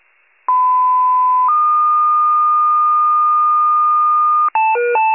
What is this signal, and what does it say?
Two-tone page, heard as a one-second tone, then a three-second tone, about 1010 then 1220 Hz. It is heard from 0.5 to 4.5 s.
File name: wcalert